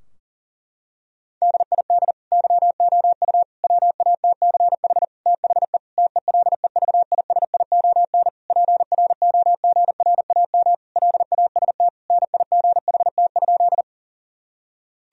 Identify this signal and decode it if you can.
Morse code, heard as a keyed tone, about 40 wpm, 700 Hz. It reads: DID YOU WATCH THE TELEVISION PROGRAM LAST NIGHT?
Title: Sentences from Top 1000 Words - Review Part 4 - 40wpm